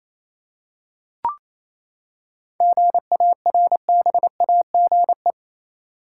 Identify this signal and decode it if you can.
Morse code, heard as a keyed tone, about 28 wpm, 700 Hz. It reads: GARBAGE
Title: Seven Letter Words - 28wpm